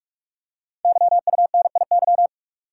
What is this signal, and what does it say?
Morse code, heard as a keyed tone, about 44 wpm, 700 Hz. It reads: YUNIY
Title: Code Groups - 5 Letters - 45wpm